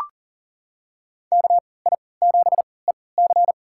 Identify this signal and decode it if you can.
Morse code, heard as a keyed tone, about 40 wpm, spaced as for 23 wpm, 700 Hz. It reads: KI7EC
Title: USA Call Signs - 5 Characters - 3x Character Spacing - 40wpm